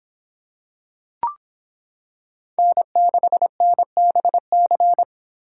Morse code, heard as a keyed tone, about 26 wpm, 700 Hz. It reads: N6NBC